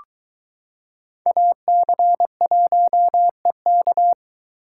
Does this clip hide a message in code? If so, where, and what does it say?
Yes: Morse code from 1.3 to 4.1 s, heard as a keyed tone, about 22 wpm, 700 Hz. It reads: AC1EK